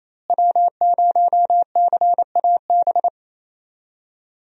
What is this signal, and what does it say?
Morse code, heard as a keyed tone, about 28 wpm, 700 Hz. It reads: W0CAB
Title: USA Call Signs - 5 Characters - Repeated 3x - 28wpm